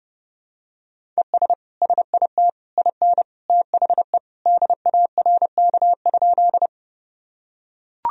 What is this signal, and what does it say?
Morse code, heard as a keyed tone, about 28 wpm, 700 Hz. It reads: ES SIT IN THE DARK?
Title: Sentences from Top 500 Words and Abbreviations 30wpm